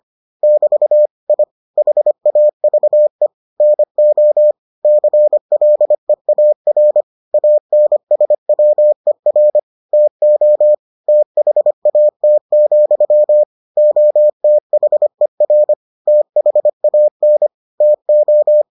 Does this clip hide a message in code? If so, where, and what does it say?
Yes: Morse code from 0.4 to 18.6 s, heard as a keyed tone, about 25 wpm, 600 Hz. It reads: <BT> I HAVE NO CLEAR ANSWER TO THAT, OTHER THAN TO